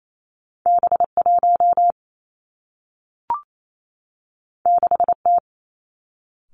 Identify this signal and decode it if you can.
Morse code, heard as a keyed tone, about 28 wpm, 700 Hz. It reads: B 1 6 T
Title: Two Challenging Letter-Number Combinations 28wpm